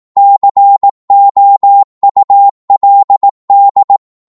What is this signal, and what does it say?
Morse code, heard as a keyed tone, about 18 wpm, 800 Hz. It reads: COULD
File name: Morse / psg-18wpm